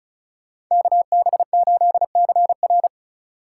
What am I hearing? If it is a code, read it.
Morse code, heard as a keyed tone, about 45 wpm, 700 Hz. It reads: KB8CR